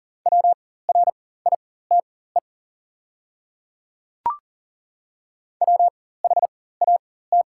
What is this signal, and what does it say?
Morse code, heard as a keyed tone, about 38 wpm, spaced as for 18 wpm, 690 Hz. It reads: WRITE WHAT
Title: 4x Character Spacing - Top 200 Words - 40wpm